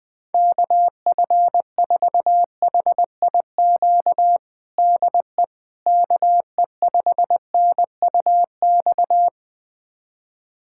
Morse code, heard as a keyed tone, about 20 wpm, 700 Hz. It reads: KF4HIQ DE KE5NUX
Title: USA Callsign DE Callsign 20wpm